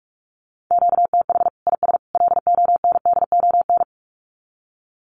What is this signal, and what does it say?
Morse code, heard as a keyed tone, about 44 wpm, 700 Hz. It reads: QTH IS LONDON
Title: Common QSO Phrases 45wpm